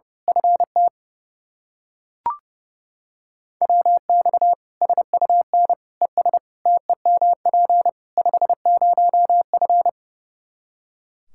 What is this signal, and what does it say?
Morse code, heard as a keyed tone, about 32 wpm, 700 Hz. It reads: FT WX SUN ES TEMP 50F
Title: Practice QSO - Exchange 2 - Info - 30wpm